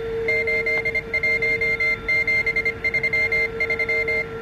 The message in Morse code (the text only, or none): 81733